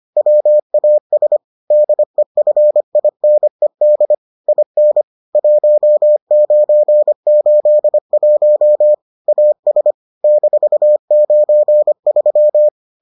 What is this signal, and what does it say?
Morse code, heard as a keyed tone, about 25 wpm, 600 Hz. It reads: WAS DEFINED IN 1981 AS -93